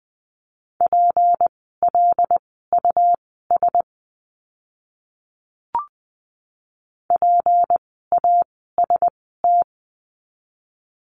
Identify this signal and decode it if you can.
Morse code, heard as a keyed tone, about 20 wpm, spaced as for 14 wpm, 700 Hz. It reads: PLUS PAST
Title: 153 Lesson 25f - Letter p - Words 20wpm